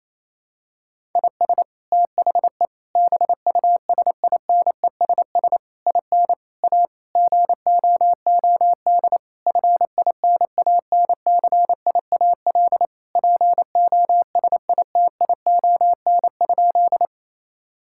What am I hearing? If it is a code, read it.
Morse code, heard as a keyed tone, about 28 wpm, 700 Hz. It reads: IS THE BUSINESS IN A GOOD FINANCIAL POSITION?
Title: Sentences from Top 1000 Words - Review Part 5 - 28wpm